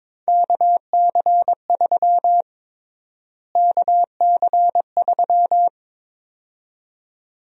Morse code, heard as a keyed tone, about 22 wpm, 700 Hz. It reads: KC3 KC3
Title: USA Call Sign Prefixes - Repeated 2x - 22wpm